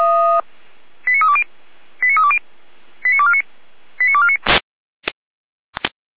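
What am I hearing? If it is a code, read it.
Five-tone selective call (ZVEI), heard as quick runs of five tones, tones of about 70 ms. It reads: 89320 89320 89249 89249
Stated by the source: ZVEI five-tone selective call received by SDR on channel 455.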